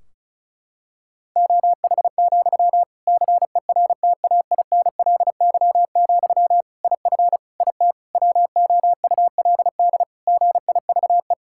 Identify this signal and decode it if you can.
Morse code, heard as a keyed tone, about 36 wpm, 700 Hz. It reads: OH, CERTAINLY, IF IT WOULD GIVE
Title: Sentences from Top 1600 Words 35wpm